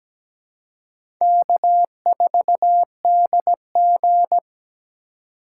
Morse code, heard as a keyed tone, about 17 wpm, 700 Hz. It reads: K4DG
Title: USA Call Signs - 4 Characters - 3x Character Spacing - 17wpm